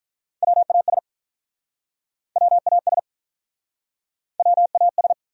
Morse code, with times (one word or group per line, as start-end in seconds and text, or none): 0.4-1.0 WAS
2.4-3.0 WAS
4.4-5.1 WAS